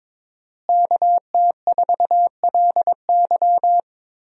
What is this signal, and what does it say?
Morse code, heard as a keyed tone, about 22 wpm, 700 Hz. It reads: KT4LY
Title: USA Call Signs - 5 Characters - Repeated 2x - 22wpm